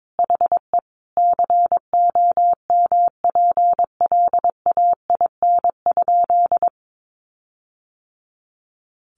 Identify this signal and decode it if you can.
Morse code, heard as a keyed tone, about 22 wpm, 700 Hz. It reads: HE COMPLAIN?
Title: Sentences from Top 1300 Words 22wpm